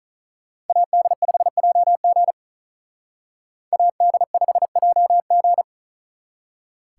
Morse code, heard as a keyed tone, about 45 wpm, 700 Hz. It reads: AD5JG AD5JG